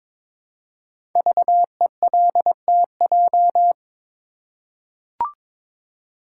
Morse code, heard as a keyed tone, about 23 wpm, 700 Hz. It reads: VELTJ
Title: Code Groups - 5 Letter:Numbers - 22wpm